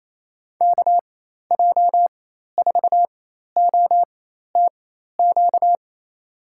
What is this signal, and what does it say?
Morse code, heard as a keyed tone, about 28 wpm, spaced as for 13 wpm, 700 Hz. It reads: KJ4OTQ
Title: USA Call Signs - 6 Characters - 4x Character Spacing - 28wpm